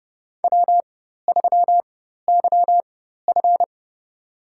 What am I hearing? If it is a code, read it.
Morse code, heard as a keyed tone, about 30 wpm, spaced as for 14 wpm, 700 Hz. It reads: W3YF
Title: USA Call Signs - 4 Characters - 4x Character Spacing - 30wpm